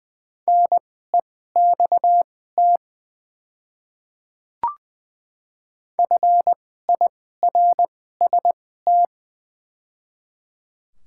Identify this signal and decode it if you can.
Morse code, heard as a keyed tone, about 21 wpm, spaced as for 15 wpm, 700 Hz. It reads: NEXT FIRST
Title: 2x Character Spacing - Top 300 Words - 20wpm